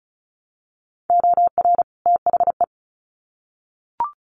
Morse code, heard as a keyed tone, about 34 wpm, 700 Hz. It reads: OR THE
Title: Sets of 2 Words 35wpm